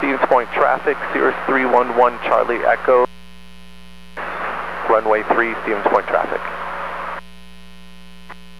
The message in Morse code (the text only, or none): none